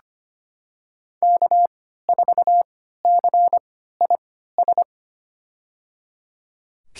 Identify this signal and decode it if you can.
Morse code, heard as a keyed tone, about 25 wpm, spaced as for 14 wpm, 700 Hz. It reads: K4CIS
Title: USA Call Signs - 3x Character Spacing - 25wpm